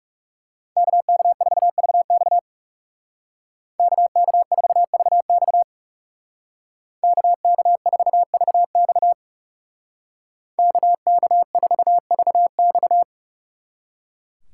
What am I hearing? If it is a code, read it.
Morse code, heard as a keyed tone, about 36 wpm, 700 Hz. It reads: KK4VX KK4VX KK4VX KK4VX
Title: Speed Racing - USA 2x2 Call Signs 30wpm